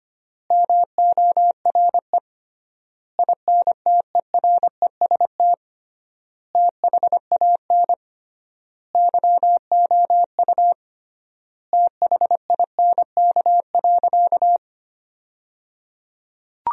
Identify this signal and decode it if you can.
Morse code, heard as a keyed tone, about 25 wpm, 700 Hz. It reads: MORE INTEREST THAN YOU THINK.
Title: 3x Word-Spacing - Sentences from Top 400 Words - 25wpm